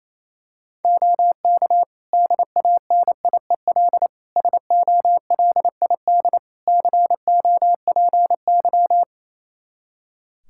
Morse code, heard as a keyed tone, about 28 wpm, 700 Hz. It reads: OK DANIEL SOLID COPY
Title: K1USN SST QSO Practice 28wpm